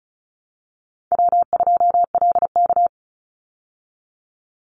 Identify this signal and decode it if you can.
Morse code, heard as a keyed tone, about 34 wpm, 700 Hz. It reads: W2LK
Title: Easy Call Signs Repeated 2x - 35wpm